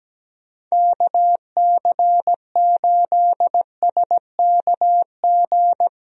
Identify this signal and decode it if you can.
Morse code, heard as a keyed tone, about 17 wpm, 700 Hz. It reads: KC8SKG
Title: USA Call Signs - 6 Characters - Repeated 2x - 17wpm